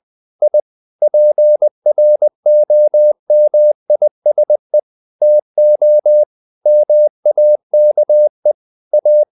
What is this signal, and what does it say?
Morse code, heard as a keyed tone, about 20 wpm, 600 Hz. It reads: I PROMISE TO MAKE A
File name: the old man and the sea / the old man and the sea part 1 20wpm